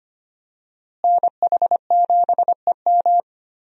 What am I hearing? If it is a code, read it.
Morse code, heard as a keyed tone, about 25 wpm, 700 Hz. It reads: NH7EM
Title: USA Call Signs - 5 Characters - Repeated 3x - 25wpm